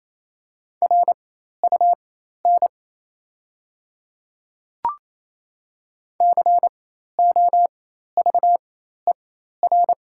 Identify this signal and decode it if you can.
Morse code, heard as a keyed tone, about 28 wpm, spaced as for 13 wpm, 700 Hz. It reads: RUN COVER
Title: 4x Character Spacing - Top 300 Words - 28wpm